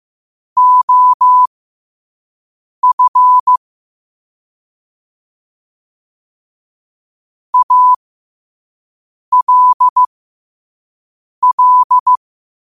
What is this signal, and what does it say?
Morse code, heard as a keyed tone, about 15 wpm, spaced as for 5 wpm, 1000 Hz. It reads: OF ALL
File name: qotdfiles / 5wpm